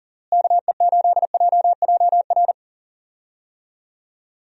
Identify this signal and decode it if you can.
Morse code, heard as a keyed tone, about 40 wpm, 700 Hz. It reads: KE8JJR
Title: USA Call Signs - 6 Characters - Repeated 3x - 40wpm